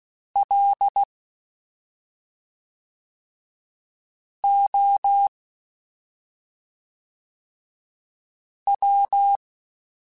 Morse code, heard as a keyed tone, about 16 wpm, 800 Hz. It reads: L O W
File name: Special INGs / Special INGs-mp3s / SlowToFaster ing @5